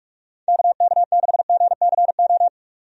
Morse code, heard as a keyed tone, about 46 wpm, 700 Hz. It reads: KK6GCO
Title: USA Call Signs - 6 Characters - 45wpm